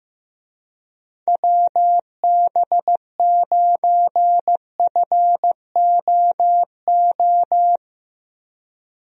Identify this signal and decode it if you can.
Morse code, heard as a keyed tone, about 16 wpm, 700 Hz. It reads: WB9FOO
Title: USA Call Signs - 6 Characters - Repeated 3x - 15wpm